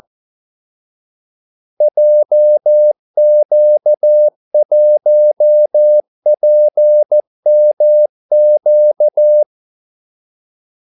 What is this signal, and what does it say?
Morse code, heard as a keyed tone, about 14 wpm, 600 Hz. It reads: JQ1PMQ
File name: Callsigns-14wpm